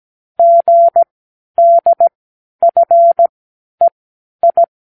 Morse code, heard as a keyed tone, about 17 wpm, spaced as for 11 wpm, 680 Hz. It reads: GDFEI
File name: k7qo / Lesson15(A-J)